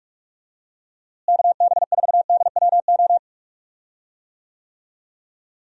Morse code, heard as a keyed tone, about 41 wpm, 700 Hz. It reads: KB4DWO
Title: USA Call Signs - 6 Characters - Repeated 2x - 45wpm